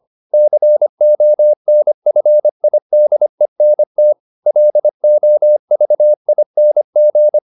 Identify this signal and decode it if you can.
Morse code, heard as a keyed tone, about 25 wpm, 600 Hz. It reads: CONFIDENT LOVING
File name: the old man and the sea part 1 25wpm